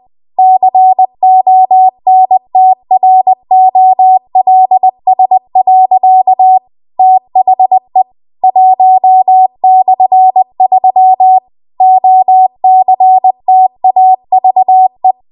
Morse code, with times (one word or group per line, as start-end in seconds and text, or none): 0.4-6.6 CONTROLS.
7.0-8.0 THE
8.4-11.4 1/3
11.8-15.1 OCTAVE